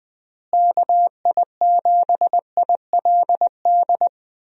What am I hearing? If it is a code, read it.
Morse code, heard as a keyed tone, about 20 wpm, 700 Hz. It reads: KI7ILD
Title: USA Call Signs - 6 Characters - Repeated 3x - 20wpm